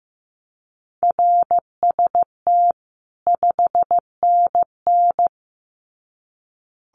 Morse code, heard as a keyed tone, about 15 wpm, 700 Hz. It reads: RST 5NN
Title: Common QSO Phrases 15wpm